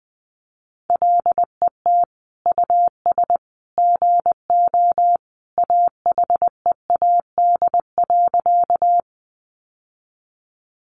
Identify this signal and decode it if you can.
Morse code, heard as a keyed tone, about 19 wpm, 700 Hz. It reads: LET US GO AHEAD.
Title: Sentences from Top 1100 Words 20wpm